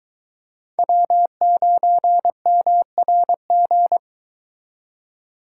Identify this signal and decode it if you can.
Morse code, heard as a keyed tone, about 23 wpm, 700 Hz. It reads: W9MRG